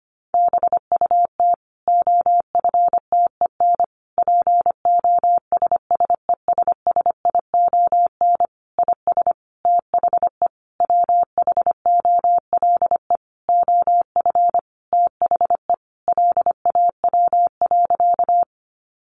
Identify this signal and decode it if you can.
Morse code, heard as a keyed tone, about 25 wpm, 700 Hz. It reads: BUT OFTEN POSSESSION IS THE WHOLE OF THE LAW.
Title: Sentences from Top 1400 Words 25wpm